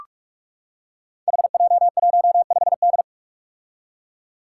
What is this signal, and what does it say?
Morse code, heard as a keyed tone, about 45 wpm, 700 Hz. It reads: HJ15D
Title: Five Challenging Letter-Number Combinations 45wpm